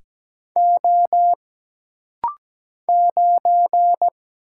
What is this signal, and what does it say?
Morse code, heard as a keyed tone, about 17 wpm, 700 Hz. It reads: O 9
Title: Single Letter-Number Alt - Rapid-Fire 17wpm